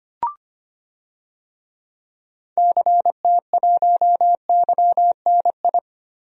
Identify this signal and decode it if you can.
Morse code, heard as a keyed tone, about 25 wpm, 700 Hz. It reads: CT1YNI
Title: DXCC-100 Call Signs - 6 Characters - Repeated 2x - 25wpm